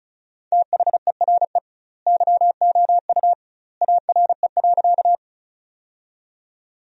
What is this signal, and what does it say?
Morse code, heard as a keyed tone, about 36 wpm, 700 Hz. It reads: THERE YOU ARE.
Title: 2x Word-Spacing - Sentences from Top 100 Words - 35wpm